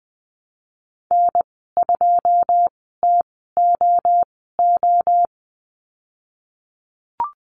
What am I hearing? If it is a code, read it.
Morse code, heard as a keyed tone, about 20 wpm, spaced as for 15 wpm, 700 Hz. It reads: N2TOO
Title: 122 Lesson 20g - Number 3 - Call Signs 20wpm